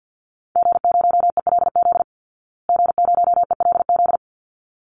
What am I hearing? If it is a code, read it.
Morse code, heard as a keyed tone, about 48 wpm, 700 Hz. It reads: G0ELZ G0ELZ
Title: Call Signs Repeated 3x - 50wpm